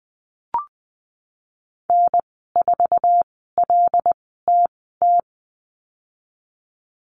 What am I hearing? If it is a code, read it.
Morse code, heard as a keyed tone, about 20 wpm, spaced as for 15 wpm, 700 Hz. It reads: N4LTT
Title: 088 Lesson 15g - Number 5 - Call Signs 20wpm